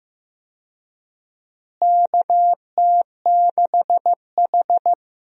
Morse code, heard as a keyed tone, about 15 wpm, 700 Hz. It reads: KT6H
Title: USA Call Signs - 4 Characters - Repeated 2x - 15wpm